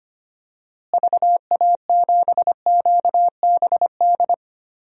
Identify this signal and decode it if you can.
Morse code, heard as a keyed tone, about 25 wpm, 700 Hz. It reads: VA7QBD
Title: DXCC-100 Call Signs - 6 Characters - Repeated 3x - 25wpm